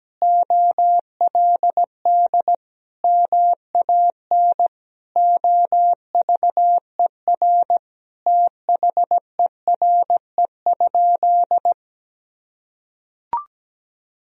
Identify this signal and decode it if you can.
Morse code, heard as a keyed tone, about 17 wpm, 700 Hz. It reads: OLD MAN OVER THERE?
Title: Sentences from Top 200 Words and Abbreviations 17wpm